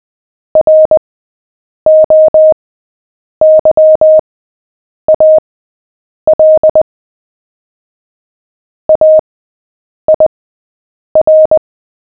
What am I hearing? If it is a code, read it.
Morse code, heard as a keyed tone, about 20 wpm, spaced as for 8 wpm, 620 Hz. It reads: ROYAL AIR
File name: cw / fists / Short-Story-7-8wpm